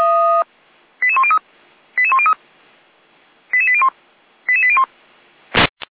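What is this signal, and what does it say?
Five-tone selective call (ZVEI), heard as quick runs of five tones, tones of about 70 ms. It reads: 88103 88103 88801 88801
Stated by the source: ZVEI five-tone selective call received on SDR.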